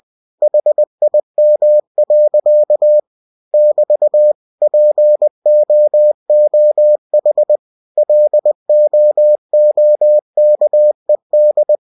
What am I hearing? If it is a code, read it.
Morse code, heard as a keyed tone, about 20 wpm, 600 Hz. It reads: HIM. <BT> POOH LOOKED
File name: cw-books / winnie-the-pooh / winnie-20wpm-page-0089